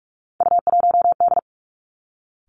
Morse code, heard as a keyed tone, about 46 wpm, 700 Hz. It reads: U1D
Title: Three Challenging Letter-Number Combinations 45wpm